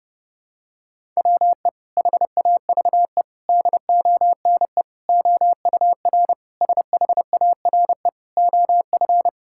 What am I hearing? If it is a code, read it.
Morse code, heard as a keyed tone, about 30 wpm, 700 Hz. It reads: WE HAVE DONE OUR SHARE OF
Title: Sentences from Top 1000 Words - Review Part 6 - 30wpm